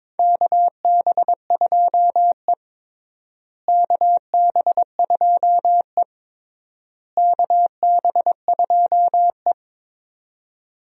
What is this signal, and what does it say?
Morse code, heard as a keyed tone, about 21 wpm, 700 Hz. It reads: KB2E KB2E KB2E
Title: USA Call Signs - 4 Characters - Repeated 3x - 22wpm